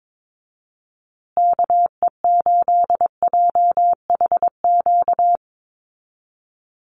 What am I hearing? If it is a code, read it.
Morse code, heard as a keyed tone, about 22 wpm, 700 Hz. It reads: KE8JHQ
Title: USA Call Signs Repeated 2x 22wpm